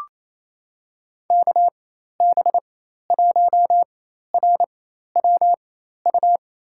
Morse code, heard as a keyed tone, about 28 wpm, spaced as for 13 wpm, 700 Hz. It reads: KB1RWU
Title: USA Call Signs - 4x Character Spacing - 28wpm